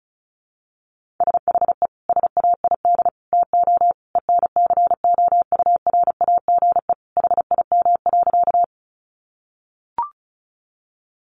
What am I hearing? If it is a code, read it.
Morse code, heard as a keyed tone, about 34 wpm, 700 Hz. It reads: SHE SAID TO ENCOURAGE HIM.
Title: Sentences from Top 1400 Words 35wpm